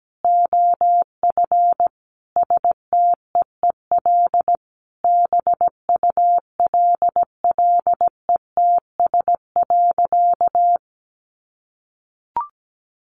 Morse code, heard as a keyed tone, about 17 wpm, 700 Hz. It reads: OF STEEL BULLETS.
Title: A Sound Of Thunder 17wpm